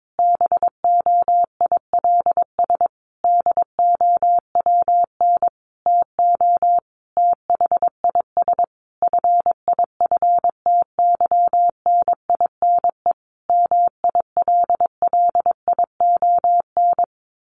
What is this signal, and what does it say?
Morse code, heard as a keyed tone, about 22 wpm, 700 Hz. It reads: BOILS DOWN TO THIS FIFTYNINE MILLION
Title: A Sound Of Thunder 22wpm